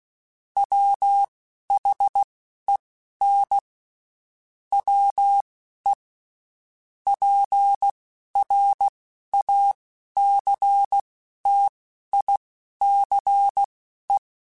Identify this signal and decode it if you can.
Morse code, heard as a keyed tone, about 16 wpm, spaced as for 11 wpm, 800 Hz. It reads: WHEN WE PRACTICE
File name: Quotes Spaced / Quotes Spaced-mp3s / Deceive sp @15